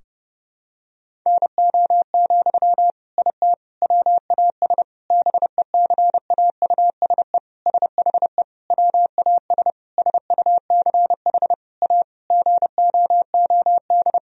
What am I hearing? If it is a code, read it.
Morse code, heard as a keyed tone, about 29 wpm, 700 Hz. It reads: NO, IT WAS BECAUSE SHE WAS SUCH A GOOD
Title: Sentences from Top 1000 Words - Review Part 4 - 30wpm